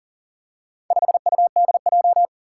Morse code, heard as a keyed tone, about 40 wpm, 700 Hz. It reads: 5UDJ
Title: Four Challenging Letter-Number Combinations 40wpm